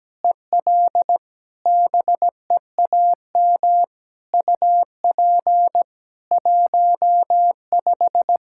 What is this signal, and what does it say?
Morse code, heard as a keyed tone, about 17 wpm, 700 Hz. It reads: EL BEAM UP 15
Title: K1USN SST QSO Practice 17wpm